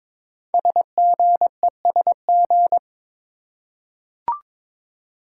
Morse code, heard as a keyed tone, about 22 wpm, 700 Hz. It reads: SGESG
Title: Code Groups - 5 Letters - 22wpm